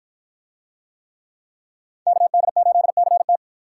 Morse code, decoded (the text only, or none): KD8CT